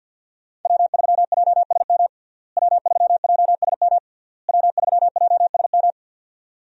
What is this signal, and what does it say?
Morse code, heard as a keyed tone, about 51 wpm, 700 Hz. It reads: W3JSM W3JSM W3JSM
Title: USA Call Signs - 5 Characters - Repeated 3x - 50wpm